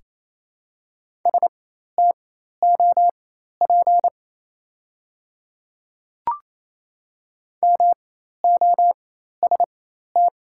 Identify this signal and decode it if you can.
Morse code, heard as a keyed tone, about 28 wpm, spaced as for 13 wpm, 700 Hz. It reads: STOP MOST
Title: 4x Character Spacing - Top 500 Words - 28wpm